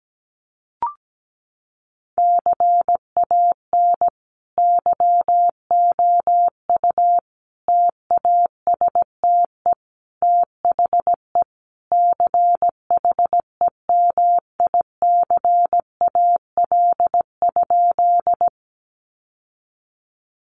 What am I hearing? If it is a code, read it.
Morse code, heard as a keyed tone, about 17 wpm, 700 Hz. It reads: CAN YOU TASTE THE CHEMICAL?
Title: Sentences from Top 1200 Words 17wpm